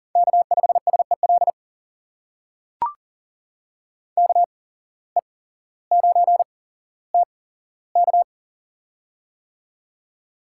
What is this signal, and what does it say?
Morse code, heard as a keyed tone, about 40 wpm, spaced as for 17 wpm, 700 Hz. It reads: K5SEL KE9TK
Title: USA Call Signs - 5 Characters - 8x Character Spacing - 40wpm